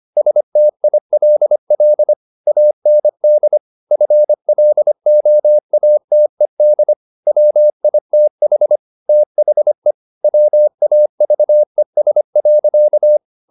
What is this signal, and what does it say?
Morse code, heard as a keyed tone, about 25 wpm, 600 Hz. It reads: STILL AND FLOATED WITH THE WAVES.